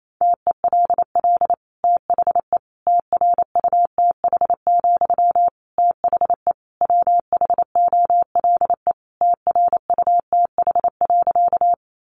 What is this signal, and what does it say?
Morse code, heard as a keyed tone, about 28 wpm, 700 Hz. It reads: TELL THE TRUTH, THE WHOLE TRUTH.
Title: Sentences from Top 800 Words 28wpm